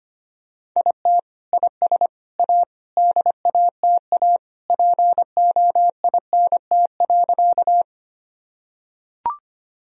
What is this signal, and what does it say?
Morse code, heard as a keyed tone, about 25 wpm, 700 Hz. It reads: IT IS A DATA POINT.
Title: Sentences from Top 1000 Words - Review Part 3 - 25wpm